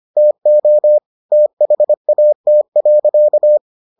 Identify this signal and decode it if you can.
Morse code, heard as a keyed tone, about 25 wpm, 600 Hz. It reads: TO THAT.